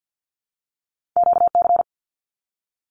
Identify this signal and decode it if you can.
Morse code, heard as a keyed tone, about 50 wpm, 700 Hz. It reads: QC
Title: Canadian Provinces and Territories - Repeated 3x - 50wpm